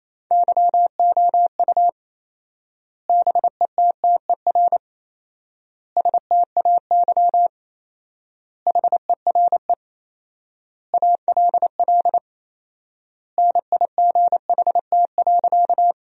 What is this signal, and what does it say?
Morse code, heard as a keyed tone, about 28 wpm, 700 Hz. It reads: YOU BETTER STAY HERE ALL NIGHT.
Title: 4x Word-Spacing - Sentences from Top 500 Words - 28wpm